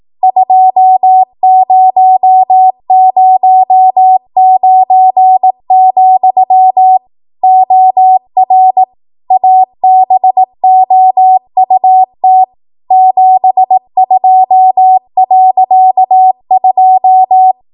Morse code, heard as a keyed tone, about 18 wpm, 750 Hz. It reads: 2009, OR ABOUT 72.2
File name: Archive / 18 WPM / 18WPM